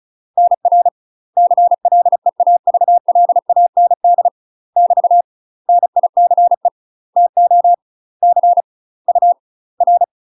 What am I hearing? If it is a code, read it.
Morse code, heard as a keyed tone, about 34 wpm, 700 Hz. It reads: NR CLEAVLAND <BT> NICE TO C U R
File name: UK MASTER QSOs TRAINING / 35WPM / UK MASTER QSOs 0034